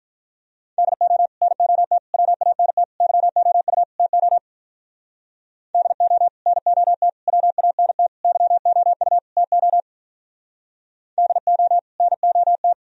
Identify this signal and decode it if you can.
Morse code, heard as a keyed tone, about 46 wpm, 700 Hz. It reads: DO NOT WANT YOU TO DO NOT WANT YOU TO DO NOT